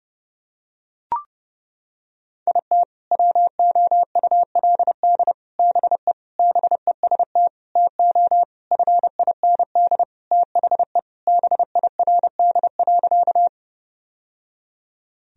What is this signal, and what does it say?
Morse code, heard as a keyed tone, about 32 wpm, 700 Hz. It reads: IT WOULD BE BEST TO FIND THE BIRD.
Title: Sentences from Top 1000 Words - Review Part 2 - 30wpm